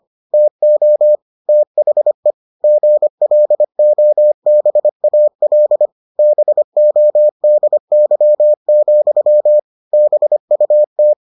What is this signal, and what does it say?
Morse code, heard as a keyed tone, about 25 wpm, 600 Hz. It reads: TO THE GLOBAL BODY, BUT